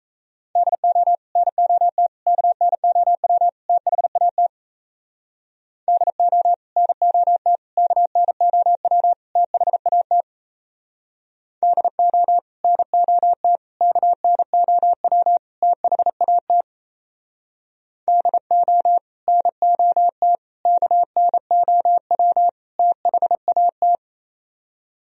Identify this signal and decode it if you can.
Morse code, heard as a keyed tone, about 35 wpm, 700 Hz. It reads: DO NOT KNOW THAT DO NOT KNOW THAT DO NOT KNOW THAT DO NOT KNOW THAT
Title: Speed-Racing - Sets of 4 Words 28wpm